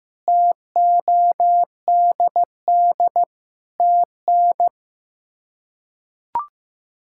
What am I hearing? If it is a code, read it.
Morse code, heard as a keyed tone, about 15 wpm, 700 Hz. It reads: TODD TN
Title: K1USN SST QSO Practice - 15wpm